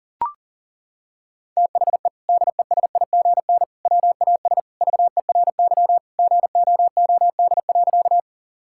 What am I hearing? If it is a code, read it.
Morse code, heard as a keyed tone, about 40 wpm, 700 Hz. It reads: THE DESIGN WAS VERY GOOD.
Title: Sentences from Top 1000 Words - Review Part 5 - 40wpm